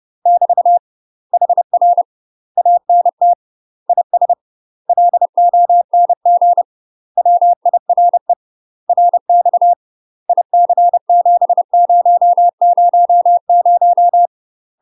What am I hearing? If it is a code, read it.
Morse code, heard as a keyed tone, about 30 wpm, 700 Hz. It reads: <BT> HR ANT IS LONG WIRE RX IC7000
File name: UK MASTER QSOs TRAINING / 30WPM / UK MASTER QSOs 0036